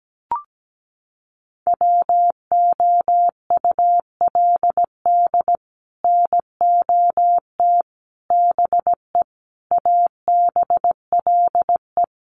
Morse code, heard as a keyed tone, about 17 wpm, 700 Hz. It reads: WOULD NOT BE ABLE
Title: Sets of 5 Words 17wpm